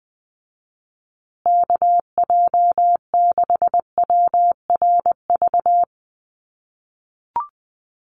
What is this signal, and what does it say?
Morse code, heard as a keyed tone, about 20 wpm, 700 Hz. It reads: KJ6WRV
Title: USA Call Signs Repeated 2x 20wpm